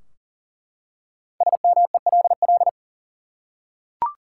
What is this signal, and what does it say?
Morse code, heard as a keyed tone, about 40 wpm, 700 Hz. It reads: SMELL
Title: Five Letter Words - 40wpm